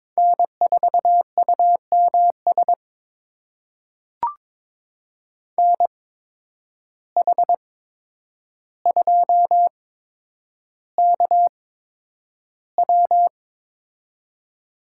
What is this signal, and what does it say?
Morse code, heard as a keyed tone, about 22 wpm, spaced as for 9 wpm, 700 Hz. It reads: N4UMS NH2KW
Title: USA Call Signs - 5 Characters - 8x Character Spacing - 22wpm